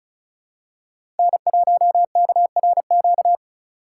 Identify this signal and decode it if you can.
Morse code, heard as a keyed tone, about 36 wpm, 700 Hz. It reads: N1KRQ
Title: USA Call Signs - 5 Characters - 2x Character Spacing - 35wpm